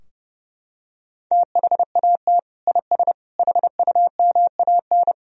Morse code, heard as a keyed tone, about 30 wpm, 700 Hz. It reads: THAT IS HUMAN
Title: Sentences from Top 1000 Words - Review Part 3 - 30wpm